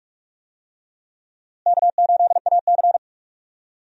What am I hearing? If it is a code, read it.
Morse code, heard as a keyed tone, about 42 wpm, 700 Hz. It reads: K8AC
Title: USA Call Signs - 4 Characters - Repeated 2x - 45wpm